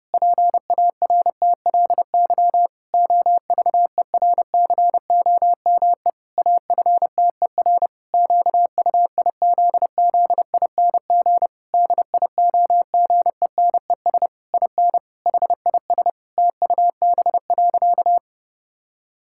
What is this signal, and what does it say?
Morse code, heard as a keyed tone, about 32 wpm, 700 Hz. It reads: PARTLY OVERCOME AFTER QUIZZING DIOGENES IN HIS TUB.
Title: Pangram Challenge v2 - 30wpm